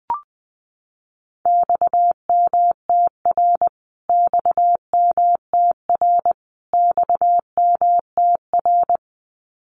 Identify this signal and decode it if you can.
Morse code, heard as a keyed tone, about 20 wpm, 700 Hz. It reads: XMTR XMTR XMTR
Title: Abbreviations Repeated 3x - 20wpm